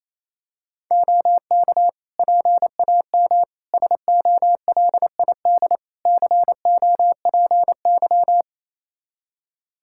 Morse code, heard as a keyed tone, about 28 wpm, 700 Hz. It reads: OK PAM SOLID COPY
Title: K1USN SST QSO Practice 28wpm